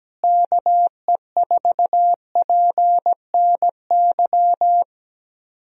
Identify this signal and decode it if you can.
Morse code, heard as a keyed tone, about 17 wpm, 700 Hz. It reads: KE4PNY
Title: USA Call Signs - 6 Characters - Repeated 3x - 17wpm